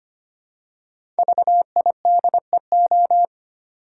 Morse code, heard as a keyed tone, about 25 wpm, 700 Hz. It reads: VIDEO
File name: 5 Letter Words - First 3 Letters Given - 25wpm